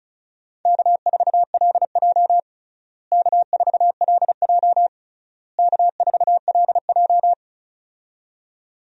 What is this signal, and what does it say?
Morse code, heard as a keyed tone, about 36 wpm, 700 Hz. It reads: K4LJ K4LJ K4LJ
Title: USA Call Signs - 4 Characters - Repeated 3x - 35wpm